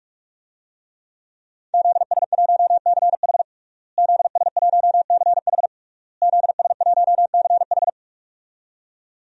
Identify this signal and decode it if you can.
Morse code, heard as a keyed tone, about 46 wpm, 700 Hz. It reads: ZS1CH ZS1CH ZS1CH
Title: DXCC-100 Call Signs - 5 Characters - Repeated 3x - 45wpm